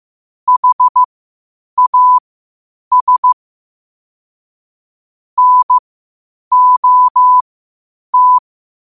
Morse code, heard as a keyed tone, about 15 wpm, spaced as for 8 wpm, 1000 Hz. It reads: HAS NOT
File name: qotdfiles / 8wpm